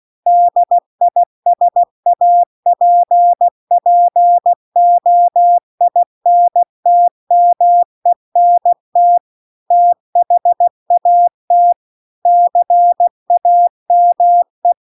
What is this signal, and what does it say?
Morse code, heard as a keyed tone, about 16 wpm, 700 Hz. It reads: DISAPPOINTMENT THAT CAME